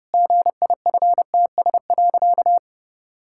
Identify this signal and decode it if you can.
Morse code, heard as a keyed tone, about 30 wpm, 700 Hz. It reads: GIFTS.
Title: Pangram Challenge v2 - 30wpm